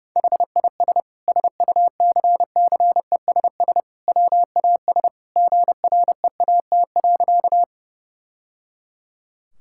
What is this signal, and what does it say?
Morse code, heard as a keyed tone, about 30 wpm, 700 Hz. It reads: HIS SUCCESS WAS GREAT.
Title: Sentences from Top 1000 Words - Review Part 6 - 30wpm